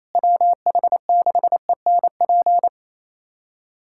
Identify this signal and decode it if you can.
Morse code, heard as a keyed tone, about 27 wpm, 700 Hz. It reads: WH6ENP
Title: USA Callsign DE Callsign - 6 Characters - 28wpm